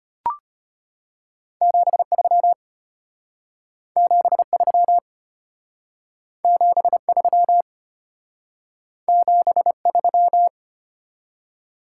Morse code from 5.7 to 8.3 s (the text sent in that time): 73